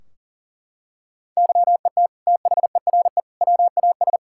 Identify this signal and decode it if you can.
Morse code, heard as a keyed tone, about 40 wpm, 700 Hz. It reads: YET THERE WAS
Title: Sentences from Top 1600 Words 40wpm